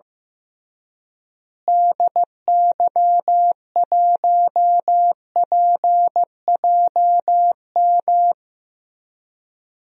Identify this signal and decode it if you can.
Morse code, heard as a keyed tone, about 15 wpm, 700 Hz. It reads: DY1PJM
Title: DXCC-100 Call Signs - 6 Characters - Repeated 3x - 15wpm